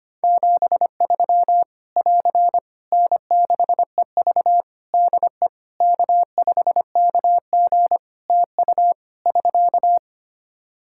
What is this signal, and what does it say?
Morse code, heard as a keyed tone, about 25 wpm, 700 Hz. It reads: 73 <AR> N6EV DE K5KG TU <SK>
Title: K1USN SST QSO Practice 25wpm